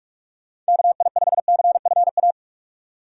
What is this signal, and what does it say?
Morse code, heard as a keyed tone, about 60 wpm, spaced as for 52 wpm, 700 Hz. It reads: KI5CFA
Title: Speed Racing - USA 2x3 Call Signs 45wpm